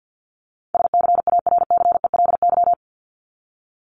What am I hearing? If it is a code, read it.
Morse code, heard as a keyed tone, about 50 wpm, 700 Hz. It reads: SCARCELY
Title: LY Suffix - Top 100 Words with Base Words 50wpm